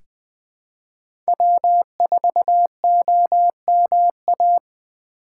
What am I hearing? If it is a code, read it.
Morse code, heard as a keyed tone, about 20 wpm, 700 Hz. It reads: W4OMA
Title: USA Call Signs - 2x Character Spacing - 20wpm